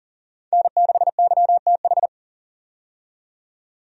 Morse code, heard as a keyed tone, about 39 wpm, 700 Hz. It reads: N6YTH
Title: USA Call Signs - 5 Characters - 40wpm